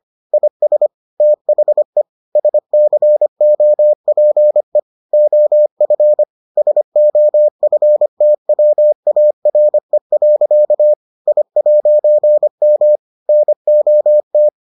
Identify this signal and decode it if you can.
Morse code, heard as a keyed tone, about 25 wpm, 600 Hz. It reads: IS THE SCOPE OF SOFTWARE. I'M NOT